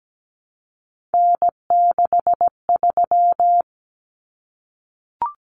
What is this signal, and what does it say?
Morse code, heard as a keyed tone, about 17 wpm, 700 Hz. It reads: N63
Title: Common Cut Numbers - 3-Digit 17wpm